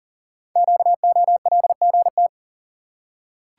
Morse code, heard as a keyed tone, about 42 wpm, 700 Hz. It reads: QOLGT
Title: Code Groups - 5 Letters - 40wpm